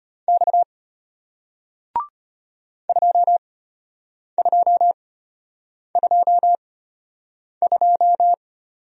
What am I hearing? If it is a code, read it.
Morse code, heard as a keyed tone, about 33 wpm, spaced as for 8 wpm, 700 Hz. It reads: X 2222